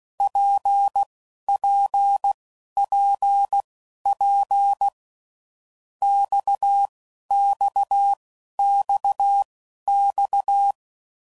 Morse code, heard as a keyed tone, about 16 wpm, spaced as for 11 wpm, 800 Hz. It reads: PPPP XXXX
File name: Groups / Basic Letter Groups / RKPX